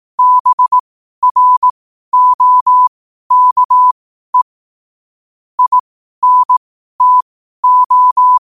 Morse code, heard as a keyed tone, about 18 wpm, spaced as for 12 wpm, 1000 Hz. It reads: BROKE INTO
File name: qotdfiles / 12wpm